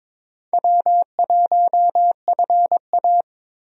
Morse code, heard as a keyed tone, about 22 wpm, 700 Hz. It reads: W1FA
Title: USA Call Signs - 4 Characters - 22wpm